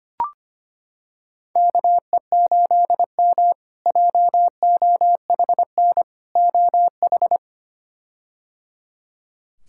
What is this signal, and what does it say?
Morse code, heard as a keyed tone, about 25 wpm, 700 Hz. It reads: KE8M JOHN OH
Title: K1USN SST QSO Practice - 25wpm